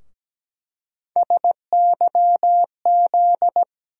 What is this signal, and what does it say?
Morse code, heard as a keyed tone, about 17 wpm, 700 Hz. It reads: SYZ
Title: USA Call Sign Postfixes - 3 Letters 17wpm